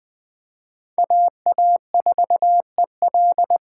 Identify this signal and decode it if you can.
Morse code, heard as a keyed tone, about 21 wpm, 700 Hz. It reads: AA4EL
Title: USA Call Signs - 3x Character Spacing - 20wpm